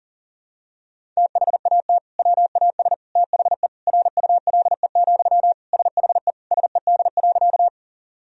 Morse code, heard as a keyed tone, about 40 wpm, 700 Hz. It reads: THAT WAS THE RULE, SHE SED.
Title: Sentences from Top 400 Words and Abbreviations 40wpm